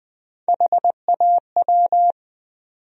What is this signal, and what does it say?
Morse code, heard as a keyed tone, about 20 wpm, 700 Hz. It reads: HAW
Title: USA Call Sign Postfixes - 3 Letters 20wpm